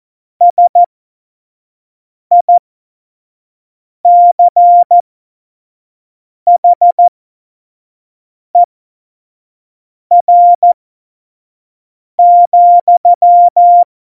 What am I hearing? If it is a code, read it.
Morse code, heard as a keyed tone, about 14 wpm, spaced as for 5 wpm, 700 Hz. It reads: SICHER,
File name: Morsekurs / Witze / Witz 22. Dezember 2020, 25 F70